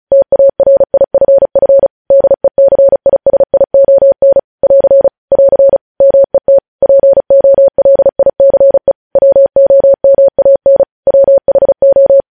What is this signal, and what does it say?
Morse code, heard as a keyed tone, about 36 wpm, 550 Hz. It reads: TARIFF DECISION <AR> <AR> MET POLICE WOMAN WHO
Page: News Headlines in Morse Code at 35 WPM